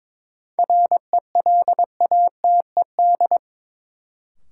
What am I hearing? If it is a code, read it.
Morse code, heard as a keyed tone, about 22 wpm, 700 Hz. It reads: RELATED
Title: Sentences from Top 1000 Words - Review Part 5 - New Word Drill - 22wpm